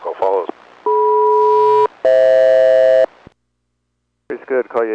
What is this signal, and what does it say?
Aircraft SELCAL, heard as two pairs of tones, with speech. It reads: DM-FH